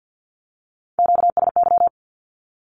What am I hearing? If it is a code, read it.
Morse code, heard as a keyed tone, about 50 wpm, 700 Hz. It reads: QSY
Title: Q-Codes 50wpm